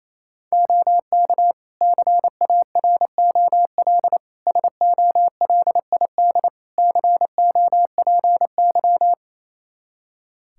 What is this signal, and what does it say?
Morse code, heard as a keyed tone, about 28 wpm, 700 Hz. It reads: OK CAROL SOLID COPY
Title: K1USN SST QSO Practice 28wpm